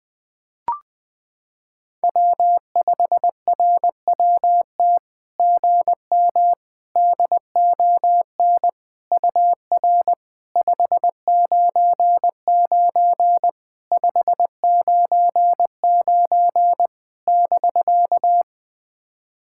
Morse code, heard as a keyed tone, about 20 wpm, 700 Hz. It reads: W5RWT GM DON UR 599 599 <BK>
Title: SOTA QSO Practice 20wpm